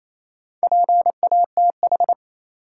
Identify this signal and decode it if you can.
Morse code, heard as a keyed tone, about 28 wpm, 700 Hz. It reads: PATH
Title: Sentences from Top 1400 Words - New Word Drill - 28wpm